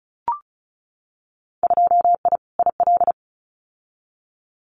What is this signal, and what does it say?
Morse code, heard as a keyed tone, about 34 wpm, 700 Hz. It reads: 2I IL
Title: Winter Field Day Exchanges 35wpm